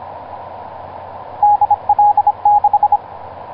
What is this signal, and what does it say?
Morse code, heard as a keyed tone, about 25 wpm, 830 Hz, with radio noise. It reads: DL6
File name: zp6cw ik1rgk etc 12m cw 19feb2010 1640